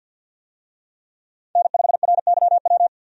Morse code, heard as a keyed tone, about 50 wpm, 700 Hz. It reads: N5RYW
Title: USA Call Signs - 5 Characters - 50wpm